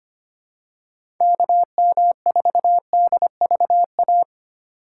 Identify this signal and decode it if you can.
Morse code, heard as a keyed tone, about 25 wpm, 700 Hz. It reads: KM4DVA
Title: USA Call Signs - 4x Character Spacing - 25wpm